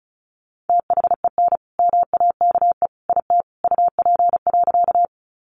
Morse code, heard as a keyed tone, about 34 wpm, 700 Hz. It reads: THEN MAKE IT UP.
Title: Sentences from Top 100 Words 35wpm